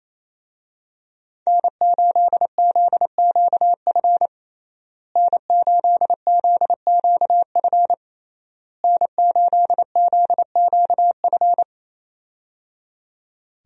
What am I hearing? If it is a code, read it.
Morse code, heard as a keyed tone, about 28 wpm, 700 Hz. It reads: N8ZQF N8ZQF N8ZQF
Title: USA Call Signs - 5 Characters - Repeated 3x - 28wpm